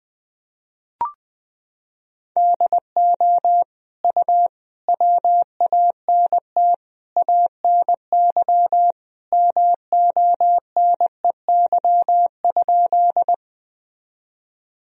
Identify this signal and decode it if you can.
Morse code, heard as a keyed tone, about 20 wpm, 700 Hz. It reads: DO U WANT ANY MONEY?
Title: Sentences from Top 400 Words and Abbreviations 20wpm